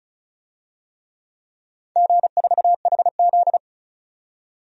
Morse code, heard as a keyed tone, about 36 wpm, 700 Hz. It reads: G4HZ
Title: DXCC-100 Call Signs - 4 Characters - 35wpm